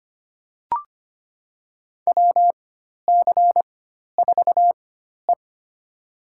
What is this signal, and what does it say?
Morse code, heard as a keyed tone, about 25 wpm, spaced as for 12 wpm, 700 Hz. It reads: WC4E
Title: USA Call Signs - 4 Characters - 4x Character Spacing - 25wpm